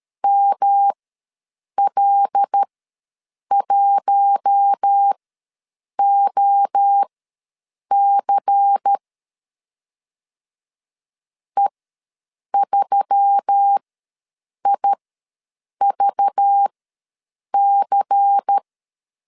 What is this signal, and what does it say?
Morse code, heard as a keyed tone, about 13 wpm, spaced as for 7 wpm, 800 Hz. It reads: ML1OC E3IVC
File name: LEZIONI / 5 LBVC / 5 LBVC ws 12wpm 3159